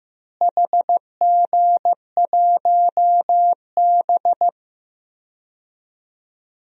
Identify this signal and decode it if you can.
Morse code, heard as a keyed tone, about 16 wpm, 700 Hz. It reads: HG1B
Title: DXCC-100 Call Signs - 4 Characters - Repeated 3x - 15wpm